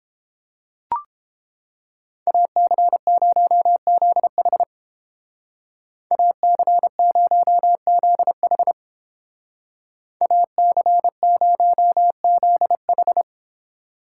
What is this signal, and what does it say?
Morse code, heard as a keyed tone, about 29 wpm, 700 Hz. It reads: AC0ZH AC0ZH AC0ZH